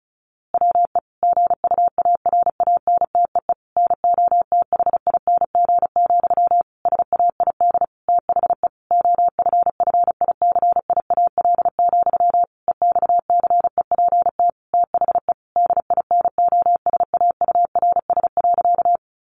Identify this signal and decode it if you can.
Morse code, heard as a keyed tone, about 34 wpm, 700 Hz. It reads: WE GUARANTEE NOTHING, SAID THE OFFICIAL, EXCEPT THE DINOSAURS.
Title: A Sound Of Thunder 35wpm